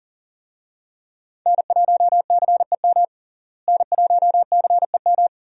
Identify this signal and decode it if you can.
Morse code, heard as a keyed tone, about 40 wpm, 700 Hz. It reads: N1CEM N1CEM
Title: USA Call Signs - 5 Characters - Repeated 2x - 40wpm